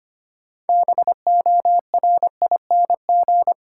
Morse code, heard as a keyed tone, about 25 wpm, 700 Hz. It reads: BORING
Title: Six Letter Words - 25wpm